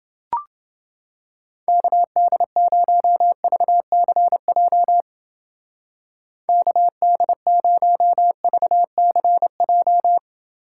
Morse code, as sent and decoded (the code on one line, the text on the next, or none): -.- -.. ----- ...- -.-. .--- / -.- -.. ----- ...- -.-. .---
KD0VCJ KD0VCJ